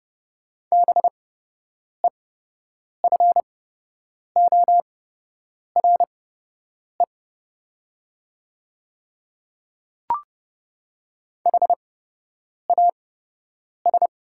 Morse code, heard as a keyed tone, about 30 wpm, spaced as for 8 wpm, 700 Hz. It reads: BEFORE HAS
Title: 8x Character Spacing - Top 200 Words - 30wpm